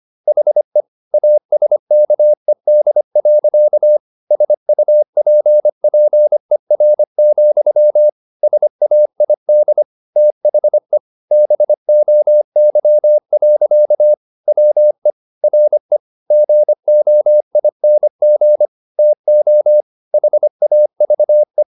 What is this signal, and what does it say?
Morse code, heard as a keyed tone, about 25 wpm, 600 Hz. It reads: HE ASKED. SUPPER, SAID THE BOY. WE RE GOING TO HAVE
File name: the old man and the sea part 1 25wpm